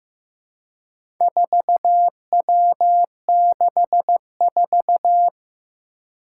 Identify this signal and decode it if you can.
Morse code, heard as a keyed tone, about 15 wpm, 700 Hz. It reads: 4W64
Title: Four Challenging Letter-Number Combinations 15wpm